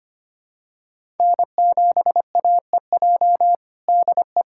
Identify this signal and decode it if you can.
Morse code, heard as a keyed tone, about 25 wpm, 700 Hz. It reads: N7AEJ DE
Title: USA Callsign DE Callsign 25wpm